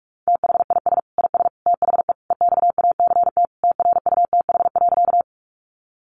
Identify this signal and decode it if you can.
Morse code, heard as a keyed tone, about 44 wpm, 700 Hz. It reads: THIS IS THE EXACT TRUTH.
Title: Sentences from Top 1300 Words 45wpm